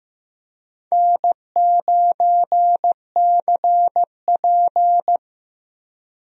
Morse code, heard as a keyed tone, about 15 wpm, 700 Hz. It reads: N9CP
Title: USA Call Signs - 4 Characters - Repeated 2x - 15wpm